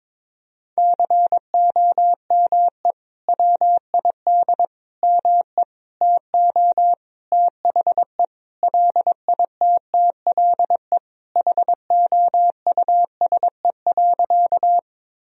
Morse code, heard as a keyed tone, about 22 wpm, 700 Hz. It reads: COME WID ME TO THE LITTLE HOUSE.
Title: Sentences from Top 200 Words and Abbreviations 22wpm